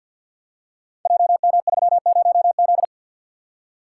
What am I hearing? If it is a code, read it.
Morse code, heard as a keyed tone, about 51 wpm, 700 Hz. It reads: JM30Z
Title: Code Groups - 5 Letter:Numbers - 50wpm